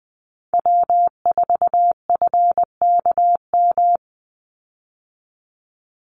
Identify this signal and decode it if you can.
Morse code, heard as a keyed tone, about 20 wpm, 700 Hz. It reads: W4FKM
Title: Call Signs Repeated 3x - 20wpm